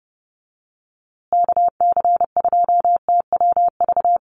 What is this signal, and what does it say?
Morse code, heard as a keyed tone, about 30 wpm, 700 Hz. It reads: KC2TWV
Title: USA Call Signs Repeated 3x 30wpm